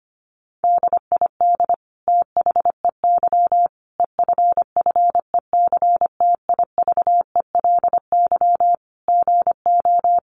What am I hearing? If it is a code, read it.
Morse code, heard as a keyed tone, about 25 wpm, 700 Hz. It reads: DID THEY EFFECTIVELY GO
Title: Sentences from Top 1400 Words 25wpm